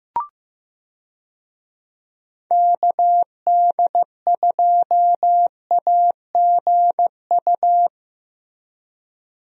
Morse code, heard as a keyed tone, about 15 wpm, 700 Hz. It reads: KD2AGU
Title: USA Call Signs - 6 Characters - Repeated 2x - 15wpm